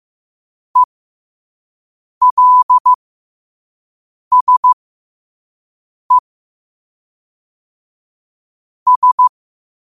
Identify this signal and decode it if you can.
Morse code, heard as a keyed tone, about 15 wpm, spaced as for 5 wpm, 1000 Hz. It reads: ELSE S
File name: qotdfiles / 5wpm